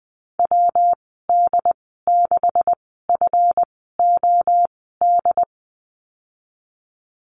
Morse code, heard as a keyed tone, about 20 wpm, spaced as for 14 wpm, 700 Hz. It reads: WD6FOD
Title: 167 Lesson 27g - Number 7 - Call Signs 20wpm